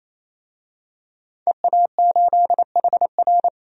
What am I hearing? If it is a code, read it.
Morse code, heard as a keyed tone, about 28 wpm, 700 Hz. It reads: EA8HR
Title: DXCC-100 Call Signs - 5 Characters - 28wpm